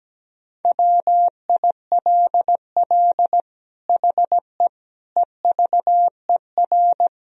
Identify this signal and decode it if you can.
Morse code, heard as a keyed tone, about 17 wpm, 700 Hz. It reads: WILL HE EVER
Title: Sentences from Top 1000 Words - Review Part 3 - 17wpm